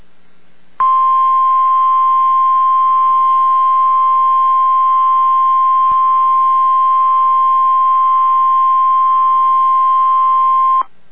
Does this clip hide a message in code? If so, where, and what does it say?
Yes: Emergency Alert System (SAME) tones (the 1050 Hz attention tone) from 0.7 to 10.8 s.